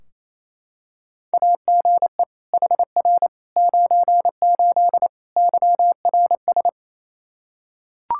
Morse code, heard as a keyed tone, about 27 wpm, 700 Hz. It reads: AGE HR 98 YRS
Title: K1USN SST QSO Practice 28wpm